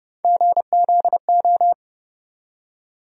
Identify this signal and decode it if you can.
Morse code, heard as a keyed tone, about 30 wpm, 700 Hz. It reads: GZO
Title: USA Call Sign Postfixes - 3 Letters 30wpm